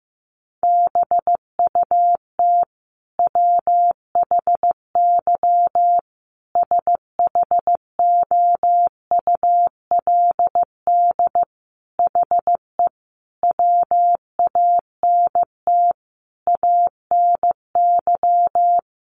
Morse code, heard as a keyed tone, about 15 wpm, 700 Hz. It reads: BUT WHY SHOULD HE WANT ANY
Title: Sentences from Top 1200 Words 15wpm